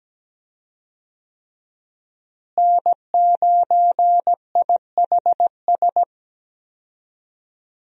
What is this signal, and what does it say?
Morse code, heard as a keyed tone, about 17 wpm, 700 Hz. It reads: N9IHS
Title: USA Call Signs - 5 Characters - Repeated 3x - 17wpm